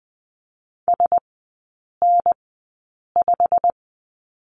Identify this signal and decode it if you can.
Morse code, heard as a keyed tone, about 20 wpm, spaced as for 8 wpm, 690 Hz. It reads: SN5
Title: 087 Lesson 15e - Number 5 - Three Characters 20wpm